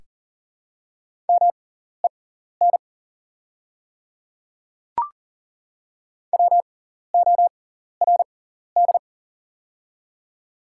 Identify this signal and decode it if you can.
Morse code, heard as a keyed tone, about 40 wpm, spaced as for 14 wpm, 700 Hz. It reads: MEN WORD
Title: 6x Character Spacing - Top 200 Words - 40wpm